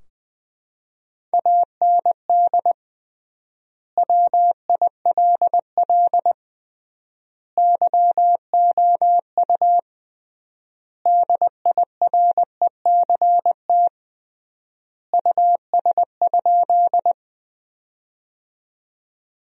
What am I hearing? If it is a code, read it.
Morse code, heard as a keyed tone, about 21 wpm, 700 Hz. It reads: AND WILL YOU DIRECT US?
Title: 3x Word-Spacing - Sentences from Top 400 Words - 20wpm